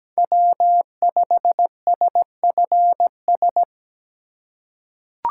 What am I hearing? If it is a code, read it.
Morse code, heard as a keyed tone, about 17 wpm, 700 Hz. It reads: W5SFS
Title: USA Call Signs - 5 Characters - 3x Character Spacing - 17wpm